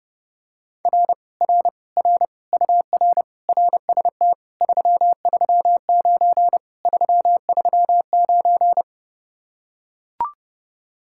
Morse code, heard as a keyed tone, about 30 wpm, 700 Hz. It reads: R R R UR RST 339 339